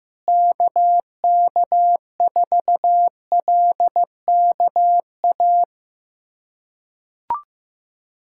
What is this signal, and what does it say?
Morse code, heard as a keyed tone, about 15 wpm, 700 Hz. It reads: KK4LKA
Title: USA Callsign DE Callsign 15wpm